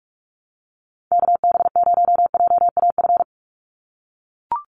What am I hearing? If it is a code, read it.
Morse code, heard as a keyed tone, about 44 wpm, 700 Hz. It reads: KB0JAF
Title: USA Call Signs 45wpm